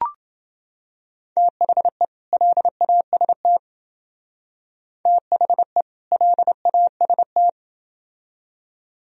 Morse code, heard as a keyed tone, about 28 wpm, 700 Hz. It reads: THE LAST THE LAST